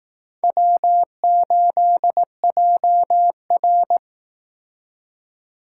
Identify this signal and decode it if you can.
Morse code, heard as a keyed tone, about 18 wpm, 700 Hz. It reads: W8JR
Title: Speed Racing - USA Call Signs 15wpm